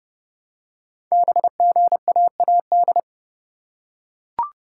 Morse code, heard as a keyed tone, about 29 wpm, 700 Hz. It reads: BGAAD
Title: Code Groups - 5 Letters - 30wpm